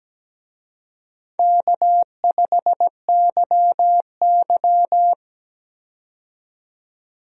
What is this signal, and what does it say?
Morse code, heard as a keyed tone, about 17 wpm, 700 Hz. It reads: K5YY
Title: USA Call Signs - 4 Characters - 17wpm